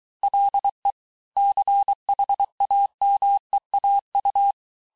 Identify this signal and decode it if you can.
Morse code, heard as a keyed tone, about 23 wpm, 800 Hz. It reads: LE CHAMEAU
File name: cours-du-mercredi / audios / COURS-855--texte-clair-vitesse-22-mpm